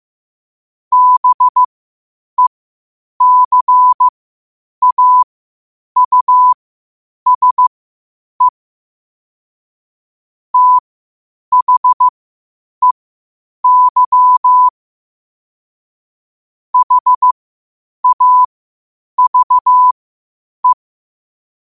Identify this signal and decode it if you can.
Morse code, heard as a keyed tone, about 15 wpm, spaced as for 8 wpm, 1000 Hz. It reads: BECAUSE THEY HAVE
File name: qotdfiles / 8wpm